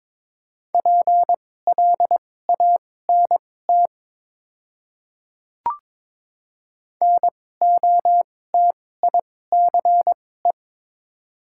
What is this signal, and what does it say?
Morse code, heard as a keyed tone, about 22 wpm, spaced as for 16 wpm, 690 Hz. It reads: PLANT NOTICE
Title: 2x Character Spacing - Top 500 Words - 22wpm